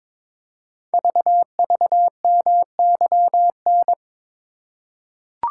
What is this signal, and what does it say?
Morse code, heard as a keyed tone, about 22 wpm, 700 Hz. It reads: VVMYN
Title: Code Groups - 5 Letters - 22wpm